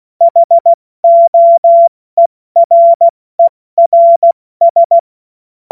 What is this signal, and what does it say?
Morse code, heard as a keyed tone, about 16 wpm, spaced as for 14 wpm, 680 Hz. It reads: HOERERS
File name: Witz-09-September-2025-70-F80